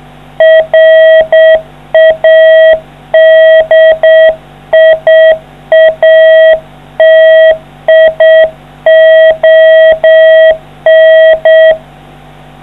Morse code, heard as a keyed tone, about 7 wpm, 640 Hz. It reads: RADIATION